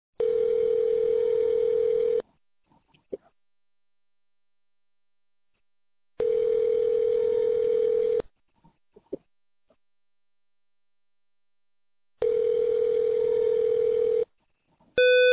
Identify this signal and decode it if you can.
Ringback tone (North America), 440 + 480 Hz.